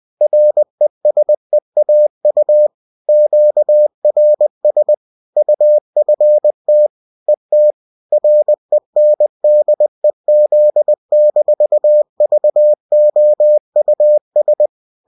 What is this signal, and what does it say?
Morse code, heard as a keyed tone, about 20 wpm, 600 Hz. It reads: RESEAU QRS UFT ET RENDEZ-VOUS